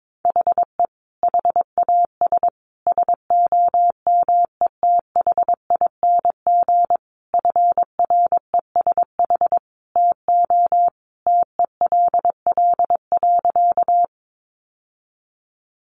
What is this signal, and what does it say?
Morse code, heard as a keyed tone, about 22 wpm, 700 Hz. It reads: HE HAS SOMETHING FRESH TO TELL.
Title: Sentences from Top 1200 Words 22wpm